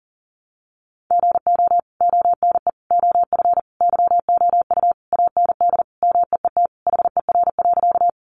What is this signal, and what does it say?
Morse code, heard as a keyed tone, about 40 wpm, 700 Hz. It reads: GO ONE OF YOU AND MEET HER.
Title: Sentences from Top 1100 Words 40wpm